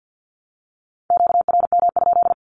Morse code, heard as a keyed tone, about 48 wpm, 700 Hz. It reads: QRM?
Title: Q-Codes 50wpm